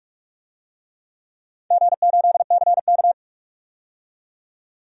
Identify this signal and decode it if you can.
Morse code, heard as a keyed tone, about 46 wpm, 700 Hz. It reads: G8CK
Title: DXCC-100 Call Signs - 4 Characters - 45wpm